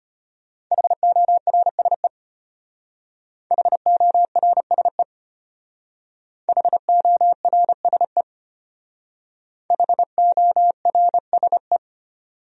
Morse code, heard as a keyed tone, about 32 wpm, 700 Hz. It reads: HORSE HORSE HORSE HORSE